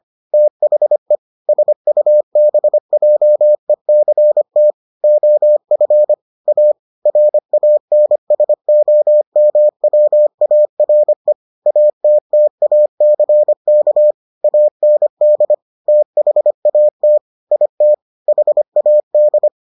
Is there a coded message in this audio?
Yes: Morse code from 0.3 to 19.5 s, heard as a keyed tone, about 25 wpm, 600 Hz. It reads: THE SUBJECT OF A RANSOMWARE ATTACK AND THAT IT HAD